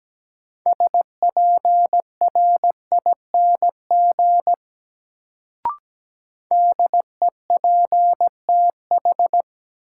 Morse code, heard as a keyed tone, about 17 wpm, 700 Hz. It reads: SPRING DEPTH
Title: Sentences from Top 1200 Words - New Word Drill - 17wpm